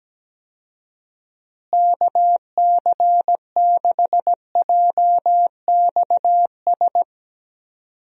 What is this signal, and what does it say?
Morse code, heard as a keyed tone, about 17 wpm, 700 Hz. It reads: KC6JXS
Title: USA Call Signs - 6 Characters - Repeated 2x - 17wpm